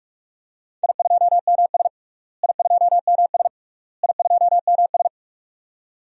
Morse code, heard as a keyed tone, about 46 wpm, 700 Hz. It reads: I2MS I2MS I2MS
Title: DXCC-100 Call Signs - 4 Characters - Repeated 3x - 45wpm